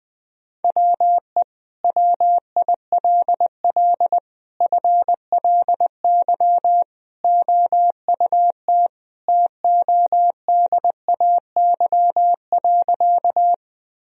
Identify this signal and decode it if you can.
Morse code, heard as a keyed tone, about 20 wpm, 700 Hz. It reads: WE WILL FLY OUT TODAY.
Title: Sentences from Top 1000 Words - Review Part 2 - 20wpm